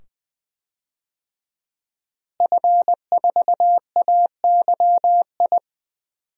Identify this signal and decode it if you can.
Morse code, heard as a keyed tone, about 20 wpm, 700 Hz. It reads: F4AYI
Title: DXCC-100 Call Signs - 5 Characters - Repeated 2x - 20wpm